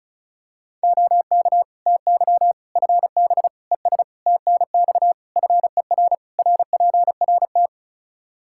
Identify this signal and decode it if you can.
Morse code, heard as a keyed tone, about 36 wpm, 700 Hz. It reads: OK TY FB ES TNX FER RPRT
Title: K1USN SST QSO Practice 35wpm